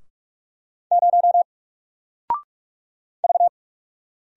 Morse code, heard as a keyed tone, about 44 wpm, 700 Hz. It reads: 0 V
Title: Single Letter-Number 45wpm